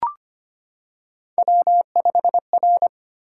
Morse code, heard as a keyed tone, about 25 wpm, 700 Hz. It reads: W5R
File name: Three Letter-Number Combinations - 25wpm